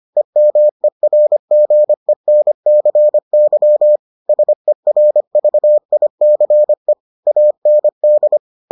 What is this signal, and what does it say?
Morse code, heard as a keyed tone, about 25 wpm, 600 Hz. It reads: EMERGENCY SERVICE AND